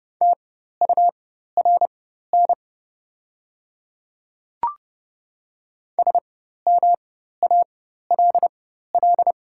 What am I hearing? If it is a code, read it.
Morse code, heard as a keyed tone, about 31 wpm, spaced as for 14 wpm, 690 Hz. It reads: TURN SMALL
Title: 4x Character Spacing - Top 300 Words - 30wpm